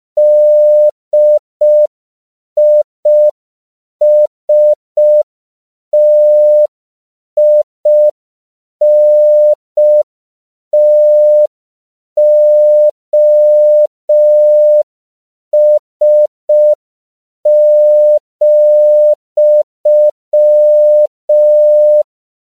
Morse code, heard as a keyed tone, about 5 wpm, 600 Hz. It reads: DISTINTOS,